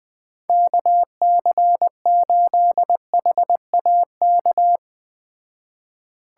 Morse code, heard as a keyed tone, about 20 wpm, 700 Hz. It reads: KC8HAK